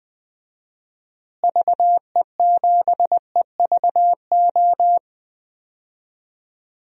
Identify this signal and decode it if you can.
Morse code, heard as a keyed tone, about 20 wpm, 700 Hz. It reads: VE7EVO
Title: DXCC-100 Call Signs - 6 Characters - Repeated 2x - 20wpm